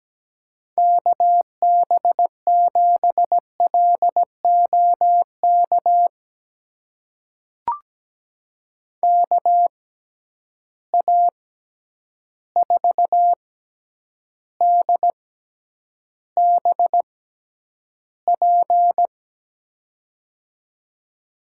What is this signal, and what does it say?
Morse code, heard as a keyed tone, about 17 wpm, spaced as for 9 wpm, 700 Hz. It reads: KB7LOK KA4DBP
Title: USA Call Signs - 6 Characters - 6x Character Spacing - 17wpm